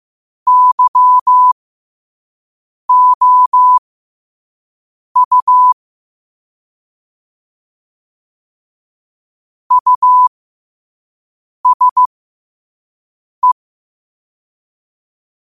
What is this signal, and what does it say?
Morse code, heard as a keyed tone, about 15 wpm, spaced as for 5 wpm, 1000 Hz. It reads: YOU USE
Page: New quotes every day in morse code at 5 Words per minute.